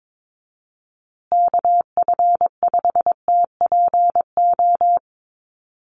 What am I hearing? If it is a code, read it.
Morse code, heard as a keyed tone, about 22 wpm, 700 Hz. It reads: KF5TPO
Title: USA Call Signs Repeated 2x 22wpm